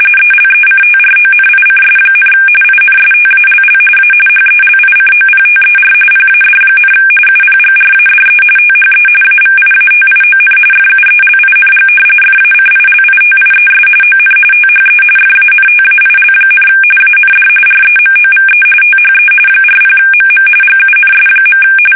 RTTY, about 45.45 baud. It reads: ELECTRONICS, INSTALL A HELICOPTER PAD, UPGRADE THE AVIATION FUELING FACILITIES ON THE FANTAIL, AND REMOVE THE 40MM GUNS.
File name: fsk850 / battleshipnj